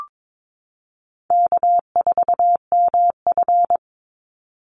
Morse code, heard as a keyed tone, about 22 wpm, 700 Hz. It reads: K4MF
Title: NA SOTA Chaser Callsigns 22wpm